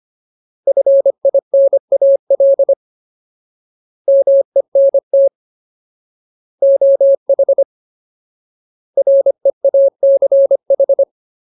Morse code, heard as a keyed tone, about 25 wpm, 550 Hz. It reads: FINAL MENT OH REACH
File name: Ultimate Word List / 25WPM Ultimate